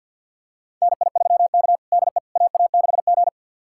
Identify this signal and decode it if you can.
Morse code, heard as a keyed tone, about 53 wpm, 700 Hz. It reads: NI3K DE AA6G
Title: USA Callsign DE Callsign - 4 Characters - 50wpm